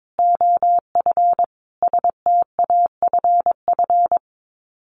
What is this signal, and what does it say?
Morse code, heard as a keyed tone, about 22 wpm, 700 Hz. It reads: OF STAFF
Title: Sets of 5 Words Encore 22wpm